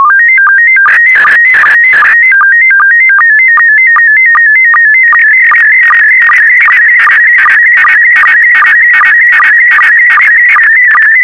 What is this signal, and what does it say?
SSTV (slow-scan television), PD 50.
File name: sstv / PD50-VK4MSL